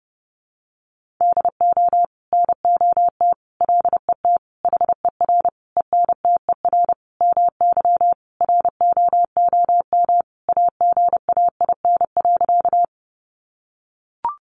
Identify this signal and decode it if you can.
Morse code, heard as a keyed tone, about 30 wpm, 700 Hz. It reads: DO NOT LET HER ENTER MY ROOM AGAIN.
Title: Sentences from Top 1300 Words 30wpm